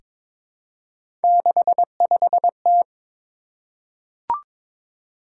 Morse code, heard as a keyed tone, about 22 wpm, 700 Hz. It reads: 65T
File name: practice / 22wpm / Three Letter-Number Combinations - 22wpm